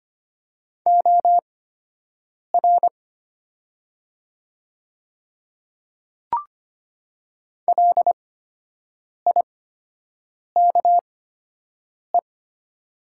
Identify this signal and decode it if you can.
Morse code, heard as a keyed tone, about 25 wpm, spaced as for 7 wpm, 700 Hz. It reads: OR LIKE
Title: 8x Character Spacing - Top 100 Words - 25wpm